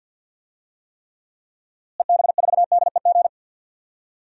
Morse code, heard as a keyed tone, about 47 wpm, 700 Hz. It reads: EB4DEG
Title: DXCC-100 Call Signs - 6 Characters - Repeated 2x - 50wpm